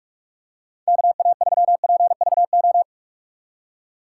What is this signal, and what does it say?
Morse code, heard as a keyed tone, about 43 wpm, 700 Hz. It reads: KA3PVO
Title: USA Call Signs - 6 Characters - 6x Character Spacing - 45wpm